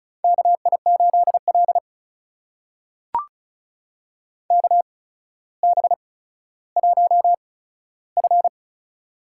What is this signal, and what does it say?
Morse code, heard as a keyed tone, about 36 wpm, spaced as for 15 wpm, 700 Hz. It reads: KI8L KB1F
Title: USA Call Signs - 4 Characters - 8x Character Spacing - 35wpm